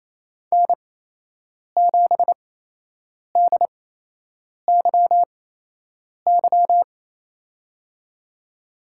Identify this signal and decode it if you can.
Morse code, heard as a keyed tone, about 28 wpm, spaced as for 8 wpm, 700 Hz. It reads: N7DYY
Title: USA Call Signs - 5 Characters - 8x Character Spacing - 28wpm